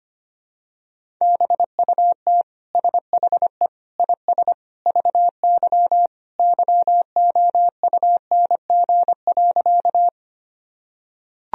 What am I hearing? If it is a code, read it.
Morse code, heard as a keyed tone, about 25 wpm, 700 Hz. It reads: BUT SHE IS VY YOUNG.
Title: Sentences from Top 400 Words and Abbreviations 25wpm